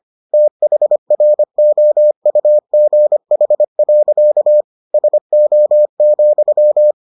Morse code, heard as a keyed tone, about 25 wpm, 600 Hz. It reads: THROUGH. SO,